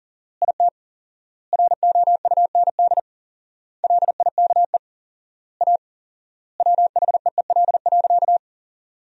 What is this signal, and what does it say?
Morse code, heard as a keyed tone, about 40 wpm, 700 Hz. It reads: IT ROUND LIKE A WHEEL.
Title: 4x Word-Spacing - Sentences from Top 500 Words - 40wpm